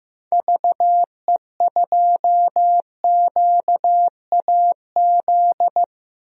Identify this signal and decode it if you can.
Morse code, heard as a keyed tone, about 16 wpm, 700 Hz. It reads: VE2QAZ
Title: DXCC-100 Call Signs - 6 Characters - Repeated 2x - 15wpm